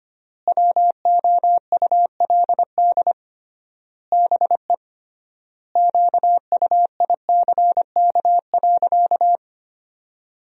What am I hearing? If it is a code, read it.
Morse code, heard as a keyed tone, about 25 wpm, 700 Hz. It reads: WOULD BE QUICK.
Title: 3x Word-Spacing - Sentences from Top 500 Words - 25wpm